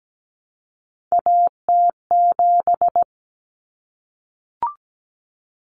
Morse code, heard as a keyed tone, about 17 wpm, 700 Hz. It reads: AT7
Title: Common Cut Numbers - 3-Digit 17wpm